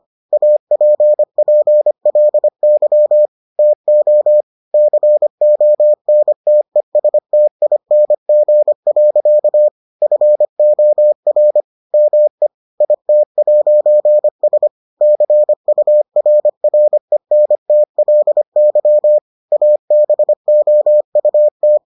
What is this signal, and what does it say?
Morse code, heard as a keyed tone, about 25 wpm, 600 Hz. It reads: APPLY TO CONTESTING. FOR ME IT'S CURRENTLY ABOUT